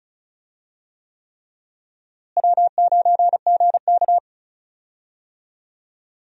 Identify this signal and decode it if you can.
Morse code, heard as a keyed tone, about 34 wpm, 700 Hz. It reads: W9GK
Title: USA Call Signs - 4 Characters - Repeated 3x - 35wpm